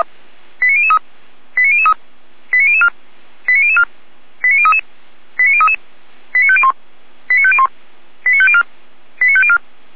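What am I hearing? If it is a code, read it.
Five-tone selective call (ZVEI), heard as quick runs of five tones, tones of about 70 ms. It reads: 89003 89003 89004 89004 89033 89033 89501 89501 89504 89504
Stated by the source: ZVEI five-tone call 89001 received by SDR, from the Feuerwehr Orlamünde audio server.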